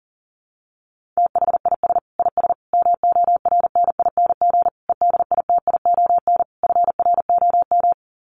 Morse code, heard as a keyed tone, about 40 wpm, 700 Hz. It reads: THIS IS MORNING EDITION FROM
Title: Sets of 5 Words Encore2 40wpm